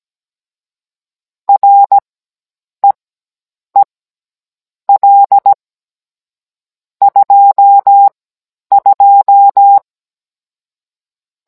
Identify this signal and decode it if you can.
Morse code, heard as a keyed tone, about 17 wpm, spaced as for 8 wpm, 800 Hz. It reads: REEL 22
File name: Cours-NR19-16mm